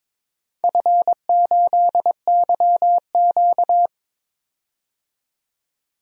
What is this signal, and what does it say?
Morse code, heard as a keyed tone, about 22 wpm, 700 Hz. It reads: F8YQ
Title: DXCC-100 Call Signs - 4 Characters - 22wpm